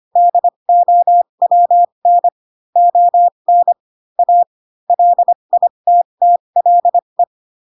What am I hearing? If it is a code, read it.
Morse code, heard as a keyed tone, about 25 wpm, spaced as for 22 wpm, 700 Hz. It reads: DOWN ON A LITTLE
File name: Pinocchio / Pinocchio ch6 18 25 ik1hge